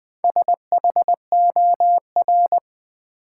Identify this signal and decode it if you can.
Morse code, heard as a keyed tone, about 20 wpm, 700 Hz. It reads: SHOR
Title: Build A Word - 1 Letter At A Time - 5-Letter Words - 20wpm